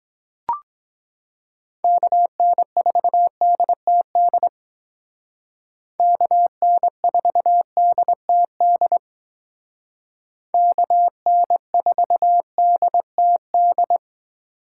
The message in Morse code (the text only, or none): KN4DTD KN4DTD KN4DTD